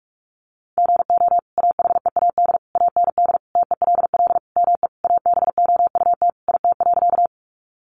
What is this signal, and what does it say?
Morse code, heard as a keyed tone, about 44 wpm, 700 Hz. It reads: GO AHEAD AND TELL ME ABOUT IT.
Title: Sentences from Top 1100 Words 45wpm